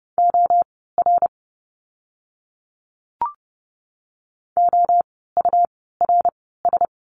Morse code, heard as a keyed tone, about 30 wpm, spaced as for 17 wpm, 700 Hz. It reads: OR OURS
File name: Lesson 17f - Letter u - Words 30wpm